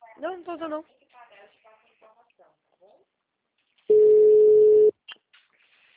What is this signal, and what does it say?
Ringback tone (Europe, 425 Hz), with speech.